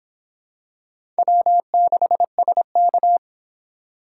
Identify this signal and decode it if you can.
Morse code, heard as a keyed tone, about 25 wpm, 700 Hz. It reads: W6SK